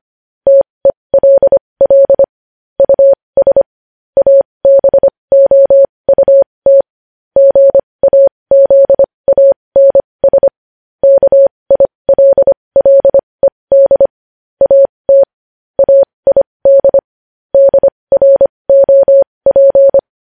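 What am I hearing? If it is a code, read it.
Morse code, heard as a keyed tone, about 25 wpm, spaced as for 20 wpm, 550 Hz. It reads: TELL US ABOUT GAZANS KILLED AT AID DROP
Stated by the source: Morse code news headlines sent at 20.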